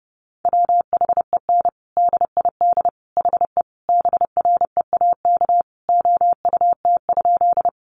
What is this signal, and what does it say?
Morse code, heard as a keyed tone, about 30 wpm, 700 Hz. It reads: WHEN DID HE BREAK OUT?
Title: Sentences from Top 900 Words 30wpm